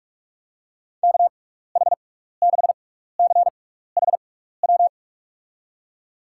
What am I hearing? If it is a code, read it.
Morse code, heard as a keyed tone, about 44 wpm, spaced as for 15 wpm, 700 Hz. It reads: KH6CHW
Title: USA Call Signs - 6x Character Spacing - 45wpm